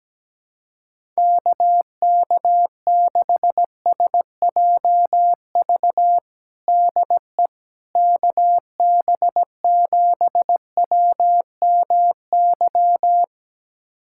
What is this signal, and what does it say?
Morse code, heard as a keyed tone, about 17 wpm, 700 Hz. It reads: KK6SJV DE KB7WMY
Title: USA Callsign DE Callsign - 6 Characters - 17wpm